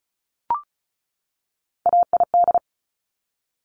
Morse code, heard as a keyed tone, about 36 wpm, 700 Hz. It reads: AID
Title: Top Three Letter Combinations in Words 35wpm